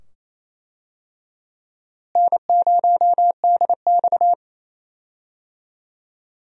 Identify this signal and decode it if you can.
Morse code, heard as a keyed tone, about 28 wpm, 700 Hz. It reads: N0DX
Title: USA Call Signs - 4 Characters - Repeated 2x - 28wpm